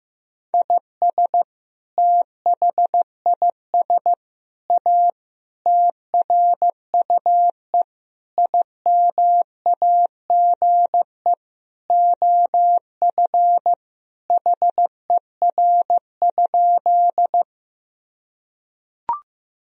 Morse code, heard as a keyed tone, about 15 wpm, 700 Hz. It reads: IS THIS A TRUE IMAGE OF HER?
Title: Sentences from Top 1000 Words - Review Part 4 - 15wpm